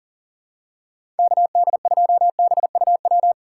Morse code, heard as a keyed tone, about 38 wpm, 700 Hz. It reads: KD2BUW
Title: USA Call Signs - 6 Characters - 2x Character Spacing - 40wpm